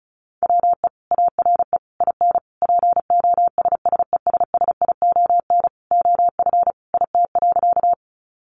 Morse code, heard as a keyed tone, about 34 wpm, 700 Hz. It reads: WE ARE IN POSSESSION OF IT.
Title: Sentences from Top 1400 Words 35wpm